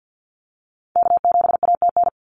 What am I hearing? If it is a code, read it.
Morse code, heard as a keyed tone, about 50 wpm, 700 Hz. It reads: K7ATN
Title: NA SOTA Activator Callsigns 50wpm